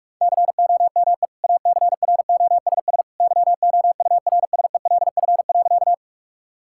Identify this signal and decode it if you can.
Morse code, heard as a keyed tone, about 46 wpm, 700 Hz. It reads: COME ACROSS YOURSELF.
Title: Sentences from Top 1000 Words - Review Part 5 - 45wpm